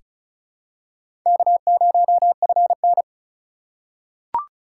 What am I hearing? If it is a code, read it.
Morse code, heard as a keyed tone, about 36 wpm, 700 Hz. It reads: K0FN
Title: USA Call Signs - 4x Character Spacing - 35wpm